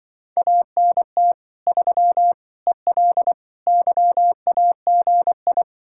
Morse code, heard as a keyed tone, about 23 wpm, 700 Hz. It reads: ANT 3 EL YAGI
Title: Speed-Racing Common QSO Phrases - 20wpm